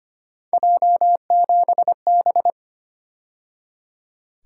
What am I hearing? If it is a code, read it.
Morse code, heard as a keyed tone, about 30 wpm, 700 Hz. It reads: J7B
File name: Speed-Racing - Three Letter-Number Combinations - 25wpm